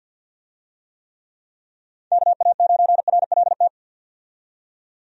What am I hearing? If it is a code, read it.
Morse code, heard as a keyed tone, about 46 wpm, 700 Hz. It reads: KA9RLT
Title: USA Call Signs - 6 Characters - 50wpm